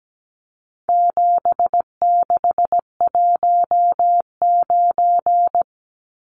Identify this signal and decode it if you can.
Morse code, heard as a keyed tone, about 17 wpm, 700 Hz. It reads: 7619
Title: Numbers - 4-Digit 17wpm